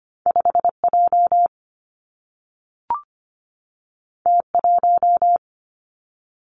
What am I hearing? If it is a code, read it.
Morse code, heard as a keyed tone, about 25 wpm, 700 Hz. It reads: 5 J T 1
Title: Two Challenging Letter-Number Combinations 25wpm